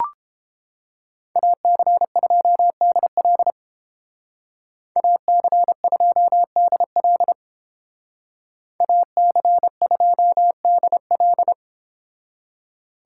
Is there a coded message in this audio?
Yes: Morse code from 1.4 to 11.5 s, heard as a keyed tone, about 30 wpm, 700 Hz. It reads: AC2DL AC2DL AC2DL